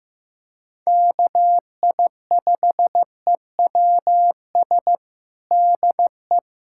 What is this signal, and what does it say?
Morse code, heard as a keyed tone, about 15 wpm, 700 Hz. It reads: KI5EWS DE
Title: USA Callsign DE Callsign 15wpm